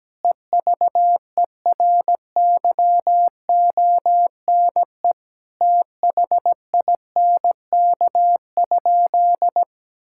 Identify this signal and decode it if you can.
Morse code, heard as a keyed tone, about 17 wpm, 700 Hz. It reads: EVERYONE THINK?
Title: Sentences from Top 1000 Words - Review Part 5 - 17wpm